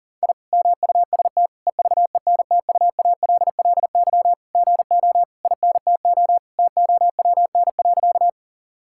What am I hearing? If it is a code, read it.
Morse code, heard as a keyed tone, about 40 wpm, 690 Hz. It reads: I MUST EVENTUALLY GO INTO TOWN.
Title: Sentences from Top 1000 Words - Review Part 6 - 40wpm